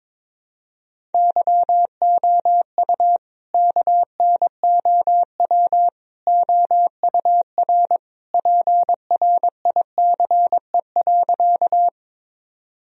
Morse code, heard as a keyed tone, about 22 wpm, 700 Hz. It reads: YOU KNOW OUR PRICE.
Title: Sentences from Top 1000 Words - Review Part 4 - 22wpm